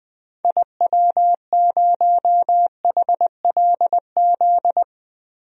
Morse code, heard as a keyed tone, about 21 wpm, 700 Hz. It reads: IW0HLZ
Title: DXCC-100 Call Signs - 6 Characters - 20wpm